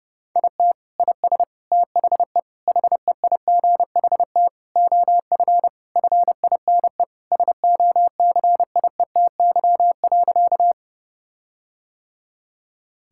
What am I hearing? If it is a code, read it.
Morse code, heard as a keyed tone, about 30 wpm, 700 Hz. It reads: IT IS THE HEIGHT OF FINE SOCIETY.
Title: Sentences from Top 1600 Words 30wpm